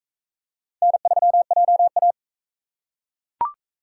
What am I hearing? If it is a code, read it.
Morse code, heard as a keyed tone, about 40 wpm, 700 Hz. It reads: N3JA